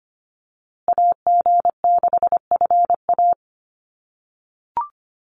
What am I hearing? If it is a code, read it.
Morse code, heard as a keyed tone, about 25 wpm, 700 Hz. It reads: AG6FA
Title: USA Call Signs Repeated 3x 25wpm